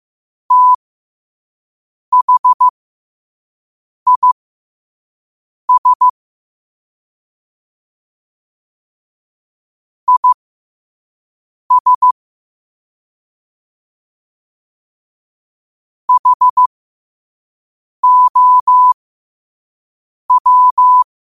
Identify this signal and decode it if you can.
Morse code, heard as a keyed tone, about 15 wpm, spaced as for 5 wpm, 1000 Hz. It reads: THIS IS HOW
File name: qotdfiles / 5wpm